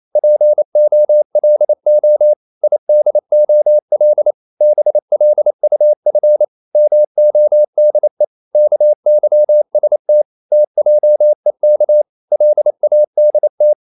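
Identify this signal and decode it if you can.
Morse code, heard as a keyed tone, about 28 wpm, 600 Hz. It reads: POLO IDOL BLUF MODE KYST TJEK LADT
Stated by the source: Danish short words sent in Morse at 28 wpm, from CW med Gnister.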